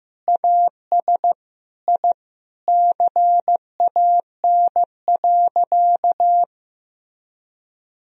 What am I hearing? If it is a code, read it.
Morse code, heard as a keyed tone, about 15 wpm, 700 Hz. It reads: AS I CAN.
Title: Sentences from Top 1000 Words - Review Part 3 - 15wpm